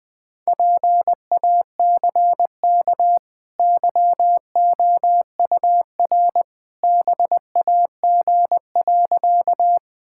Morse code, heard as a keyed tone, about 20 wpm, 700 Hz. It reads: PACK YOUR BAG.
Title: Easy Sentences with Hints - 20wpm